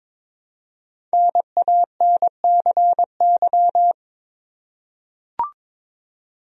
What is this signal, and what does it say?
Morse code, heard as a keyed tone, about 22 wpm, 700 Hz. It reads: NANCY
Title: USA Common Names - Max 5 Letters - 22wpm